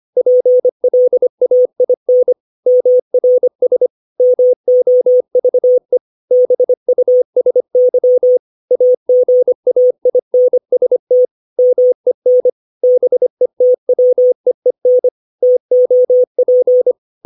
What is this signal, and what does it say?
Morse code, heard as a keyed tone, about 25 wpm, 500 Hz. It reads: PLAIN MRS MOVE BUSY AGAINST MEN BETWEEN TOP
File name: 632 words / 632 Words 25WPM